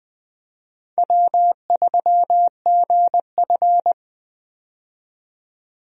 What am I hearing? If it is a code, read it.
Morse code, heard as a keyed tone, about 20 wpm, 700 Hz. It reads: W3GF
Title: USA Call Signs - 4 Characters - 20wpm